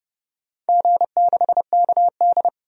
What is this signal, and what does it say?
Morse code, heard as a keyed tone, about 30 wpm, 700 Hz. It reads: G6KD
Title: DXCC-100 Call Signs - 4 Characters - 30wpm